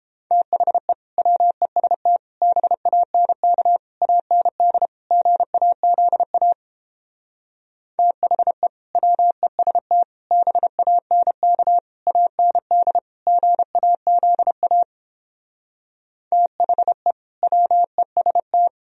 Morse code, heard as a keyed tone, about 31 wpm, 700 Hz. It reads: THE WEST BANK AND GAZA THE WEST BANK AND GAZA THE WEST